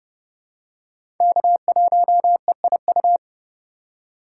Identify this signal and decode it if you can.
Morse code, heard as a keyed tone, about 30 wpm, 700 Hz. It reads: K1EIU
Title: USA Call Signs - 5 Characters - 3x Character Spacing - 30wpm